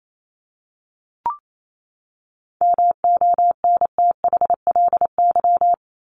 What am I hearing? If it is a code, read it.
Morse code, heard as a keyed tone, about 28 wpm, 700 Hz. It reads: MONTHLY
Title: LY Suffix - Top 100 Words with Base Words 28wpm